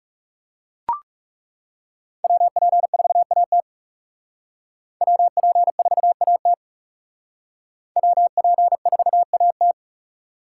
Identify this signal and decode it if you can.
Morse code, heard as a keyed tone, about 36 wpm, 700 Hz. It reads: WP4AT WP4AT WP4AT
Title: Speed Racing - USA 2x2 Call Signs 30wpm